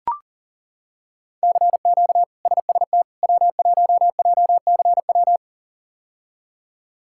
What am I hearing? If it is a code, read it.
Morse code, heard as a keyed tone, about 40 wpm, 700 Hz. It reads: CQ SST W1JCW
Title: K1USN SST QSO Practice - 40wpm